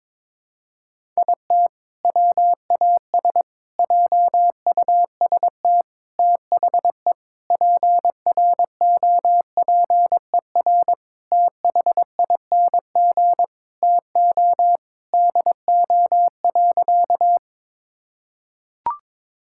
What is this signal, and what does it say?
Morse code, heard as a keyed tone, about 22 wpm, 700 Hz. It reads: IT WAS JUST THE PROPER THING TO DO.
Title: Sentences from Top 1000 Words - Review Part 5 - 22wpm